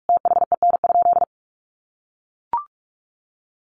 Morse code, heard as a keyed tone, about 44 wpm, 700 Hz. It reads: THEN?
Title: Sentences from Top 1300 Words 45wpm